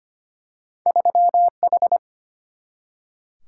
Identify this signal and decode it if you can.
Morse code, heard as a keyed tone, about 25 wpm, 700 Hz. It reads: 3H
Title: Single-Letter Single-Number 25wpm